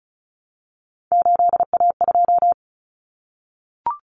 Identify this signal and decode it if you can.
Morse code, heard as a keyed tone, about 34 wpm, 700 Hz. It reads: 8A2
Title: Common Cut Numbers - 3-Digit 35wpm